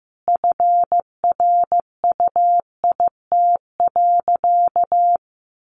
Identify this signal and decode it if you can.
Morse code, heard as a keyed tone, about 15 wpm, 700 Hz. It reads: FRUIT.
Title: Sentences from Top 1300 Words 15wpm